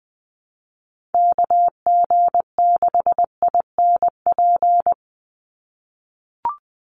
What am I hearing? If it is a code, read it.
Morse code, heard as a keyed tone, about 20 wpm, 700 Hz. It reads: KG6INP
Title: USA Call Signs 20wpm